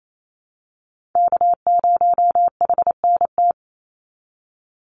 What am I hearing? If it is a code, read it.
Morse code, heard as a keyed tone, about 28 wpm, 700 Hz. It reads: K0HNT
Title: USA Call Signs 28wpm